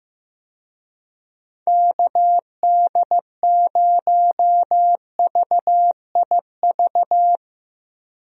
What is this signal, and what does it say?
Morse code, heard as a keyed tone, about 15 wpm, 700 Hz. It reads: KD0VIV
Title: USA Call Signs - 6 Characters - Repeated 3x - 15wpm